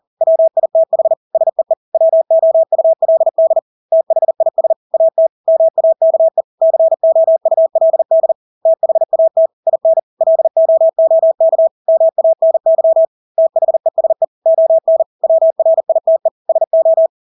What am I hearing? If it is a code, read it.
Morse code, heard as a keyed tone, about 40 wpm, 650 Hz. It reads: WITH SEE WOULD THIS AT MAKE COULD THAT IN LOOK MANY THESE ON WRITE SO
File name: Engelske ord 40-40 wpm